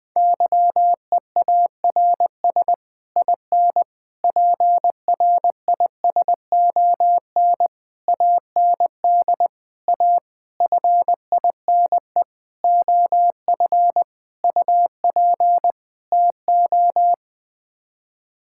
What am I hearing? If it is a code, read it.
Morse code, heard as a keyed tone, about 20 wpm, 700 Hz. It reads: YEARS IN PRISON AND A FINE OF UP TO
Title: Sets of 9 Words - 20wpm